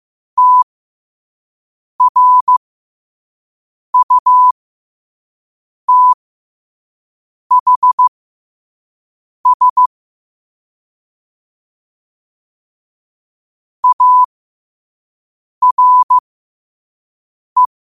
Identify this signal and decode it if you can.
Morse code, heard as a keyed tone, about 15 wpm, spaced as for 5 wpm, 1000 Hz. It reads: TRUTHS ARE
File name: qotdfiles / 5wpm